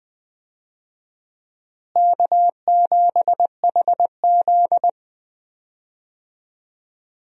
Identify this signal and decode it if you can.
Morse code, heard as a keyed tone, about 20 wpm, 700 Hz. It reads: K7HZ
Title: USA Call Signs - 4 Characters - Repeated 3x - 20wpm